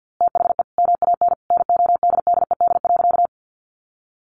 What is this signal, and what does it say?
Morse code, heard as a keyed tone, about 50 wpm, 700 Hz. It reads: THE MAN NODDED.
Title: A Sound Of Thunder 50wpm